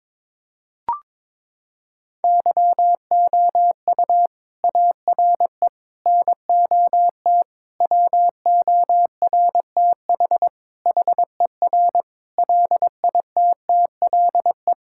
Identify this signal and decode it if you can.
Morse code, heard as a keyed tone, about 22 wpm, 700 Hz. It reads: YOU ARE NOT WORTH HER LITTLE
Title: Sentences from Top 1600 Words 22wpm